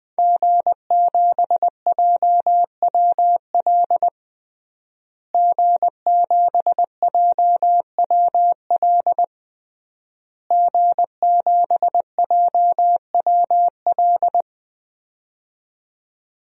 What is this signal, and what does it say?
Morse code, heard as a keyed tone, about 20 wpm, 700 Hz. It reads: G7JWL G7JWL G7JWL
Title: DXCC-100 Call Signs - 5 Characters - Repeated 3x - 20wpm